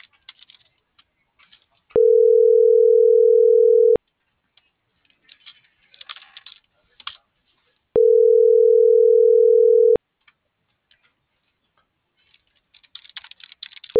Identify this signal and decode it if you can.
Ringback tone (North America), 440 + 480 Hz.